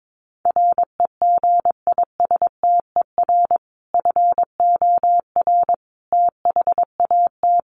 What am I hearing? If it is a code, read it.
Morse code, heard as a keyed tone, about 22 wpm, 700 Hz. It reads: REGISTER FOR THAT
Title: Sentences from Top 1200 Words 22wpm